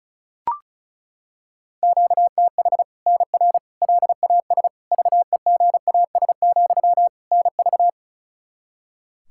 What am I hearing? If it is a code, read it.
Morse code, heard as a keyed tone, about 36 wpm, 700 Hz. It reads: QTH NR LAS VEGAS, NV
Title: Practice QSO - Exchange 1 - Beginning - 35wpm